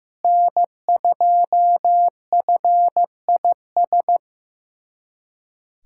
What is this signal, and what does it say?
Morse code, heard as a keyed tone, about 15 wpm, 700 Hz. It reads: N2FIS
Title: USA Callsign DE Callsign - 5 Characters - 15wpm